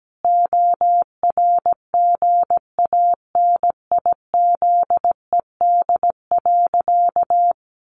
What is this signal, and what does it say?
Morse code, heard as a keyed tone, about 17 wpm, 700 Hz. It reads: ORGANIZED.
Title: Sentences from Top 1300 Words 17wpm